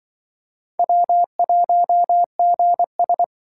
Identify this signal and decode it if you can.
Morse code, heard as a keyed tone, about 24 wpm, 700 Hz. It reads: W1GS
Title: Speed Racing - USA 1x2 Call Signs 20wpm